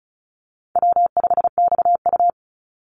Morse code, heard as a keyed tone, about 34 wpm, 700 Hz. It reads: W5XU
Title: Easy Call Signs Repeated 3x - 35wpm